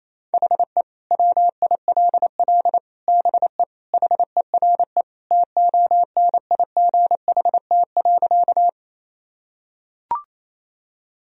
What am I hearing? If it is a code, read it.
Morse code, heard as a keyed tone, about 28 wpm, 700 Hz. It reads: HE WILL BE HERE TONIGHT.
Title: Sentences from Top 1000 Words - Review Part 3 - 28wpm